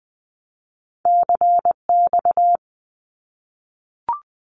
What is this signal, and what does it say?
Morse code, heard as a keyed tone, about 20 wpm, 700 Hz. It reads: CX
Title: Abbreviations Repeated 3x - 20wpm